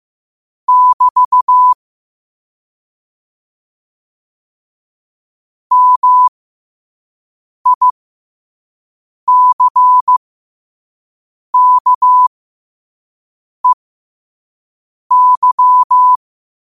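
Morse code, heard as a keyed tone, about 15 wpm, spaced as for 5 wpm, 1000 Hz. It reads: <BT> MICKEY
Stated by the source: Morse code practice at 5 wpm.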